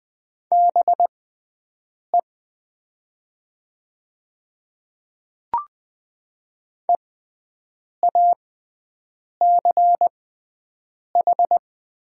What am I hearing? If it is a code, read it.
Morse code, heard as a keyed tone, about 20 wpm, spaced as for 7 wpm, 700 Hz. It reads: BE EACH
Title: 6x Character Spacing - Top 100 Words - 20wpm